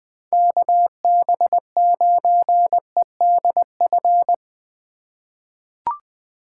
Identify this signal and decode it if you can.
Morse code, heard as a keyed tone, about 20 wpm, 700 Hz. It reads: KB9EDF
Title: USA Call Signs - 6 Characters - 2x Character Spacing - 20wpm